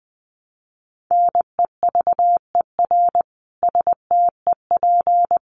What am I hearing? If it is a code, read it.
Morse code, heard as a keyed tone, about 20 wpm, 700 Hz. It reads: NEVER STEP
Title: A Sound Of Thunder 20wpm